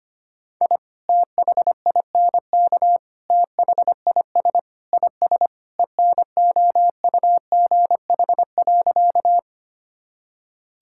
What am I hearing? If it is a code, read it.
Morse code, heard as a keyed tone, about 25 wpm, 700 Hz. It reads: I THINK THIS IS ENOUGH.
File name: practice / 25wpm / Sentences with 4-Word Phrases - 25wpm